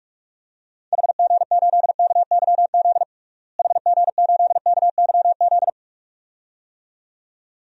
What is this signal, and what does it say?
Morse code, heard as a keyed tone, about 42 wpm, 700 Hz. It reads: HG8KYZ HG8KYZ
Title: DXCC-100 Call Signs - 6 Characters - Repeated 2x - 45wpm